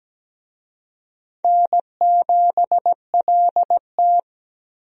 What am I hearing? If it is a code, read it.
Morse code, heard as a keyed tone, about 17 wpm, 700 Hz. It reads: N7LT
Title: USA Call Signs - 4 Characters - Repeated 2x - 17wpm